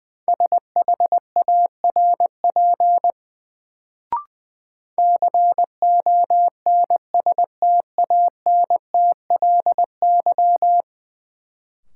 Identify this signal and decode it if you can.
Morse code, heard as a keyed tone, about 21 wpm, 700 Hz. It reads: SHARP CONSTANTLY
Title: Sentences from Top 1600 Words - New Word Drill Alt - 20wpm